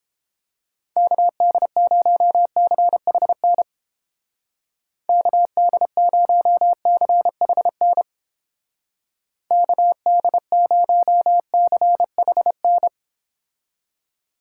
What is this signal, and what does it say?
Morse code, heard as a keyed tone, about 31 wpm, 700 Hz. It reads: KD0CHN KD0CHN KD0CHN